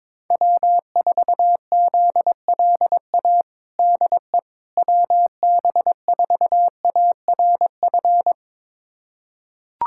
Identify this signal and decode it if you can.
Morse code, heard as a keyed tone, about 22 wpm, 700 Hz. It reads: W4ZLA DE WB4ARF
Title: USA Callsign DE Callsign 22wpm